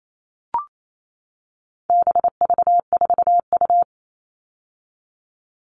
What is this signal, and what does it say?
Morse code, heard as a keyed tone, about 28 wpm, 700 Hz. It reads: BV4U
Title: All Cut Numbers - 4-Digit 28wpm